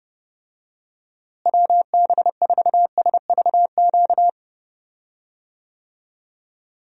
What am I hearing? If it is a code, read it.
Morse code, heard as a keyed tone, about 30 wpm, 700 Hz. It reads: WB4SVQ
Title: USA Call Signs - 6 Characters - Repeated 3x - 30wpm